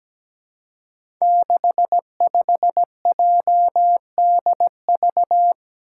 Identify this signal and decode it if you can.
Morse code, heard as a keyed tone, about 17 wpm, 700 Hz. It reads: 65JDV
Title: Five Challenging Letter-Number Combinations 17wpm